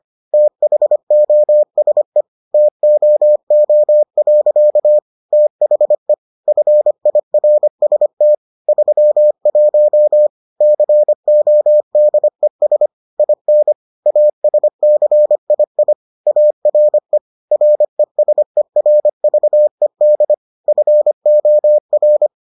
Morse code, heard as a keyed tone, about 25 wpm, 600 Hz. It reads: THOSE TOO. THE FIRST 31 CODES IN ASCII ARE RESERVED FOR